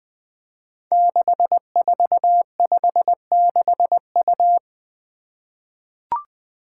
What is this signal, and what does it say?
Morse code, heard as a keyed tone, about 20 wpm, 700 Hz. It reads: 6456U
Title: Five Challenging Letter-Number Combinations 20wpm